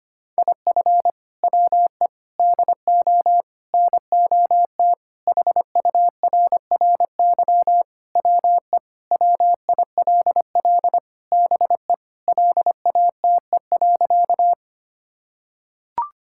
Morse code, heard as a keyed tone, about 25 wpm, 700 Hz. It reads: IF WE DO NOT HURRY WE WILL BE LATE.
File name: practice / 25wpm / Sentences with 4-Word Phrases - 25wpm